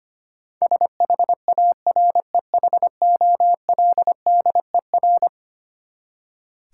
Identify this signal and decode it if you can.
Morse code, heard as a keyed tone, about 25 wpm, 700 Hz. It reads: SHAREHOLDER
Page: morse-code-ninja-practice-25wpm